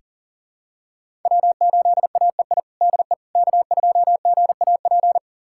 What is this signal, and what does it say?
Morse code, heard as a keyed tone, about 37 wpm, 700 Hz. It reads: W8AEI DE K2GAP
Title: USA Callsign DE Callsign - 5 Characters - 40wpm